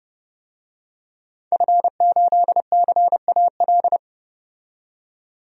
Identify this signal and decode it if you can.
Morse code, heard as a keyed tone, about 29 wpm, 700 Hz. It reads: F8CAL
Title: DXCC-100 Call Signs - 5 Characters - 30wpm